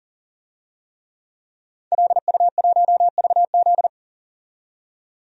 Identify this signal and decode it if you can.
Morse code, heard as a keyed tone, about 40 wpm, 700 Hz. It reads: LU1VZ
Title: DXCC-100 Call Signs - 5 Characters - 40wpm